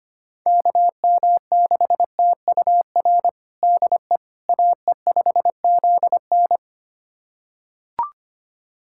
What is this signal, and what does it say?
Morse code, heard as a keyed tone, about 25 wpm, 700 Hz. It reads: KM6TUR DE AE5ZN
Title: USA Callsign DE Callsign 25wpm